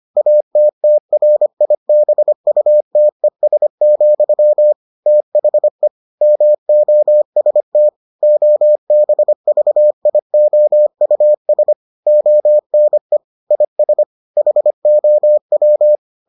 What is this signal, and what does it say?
Morse code, heard as a keyed tone, about 25 wpm, 600 Hz. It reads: ATTRIBUTES, THE MOST OBVIOUS ONE IS HOW